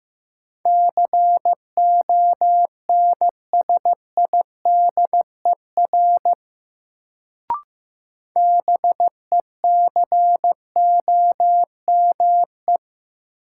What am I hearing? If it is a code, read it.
Morse code, heard as a keyed tone, about 15 wpm, 700 Hz. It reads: CONSIDER BECOME
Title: Sentences from Top 600 Words - New Word Drill - 15wpm